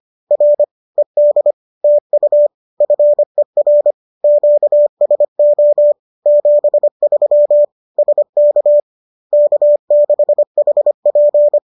Morse code, heard as a keyed tone, about 25 wpm, 590 Hz. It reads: R ED TU FER QSO 73 SK K6HP
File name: qso205 25